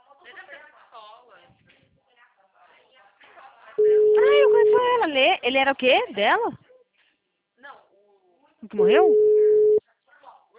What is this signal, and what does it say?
Ringback tone (Europe, 425 Hz), with speech.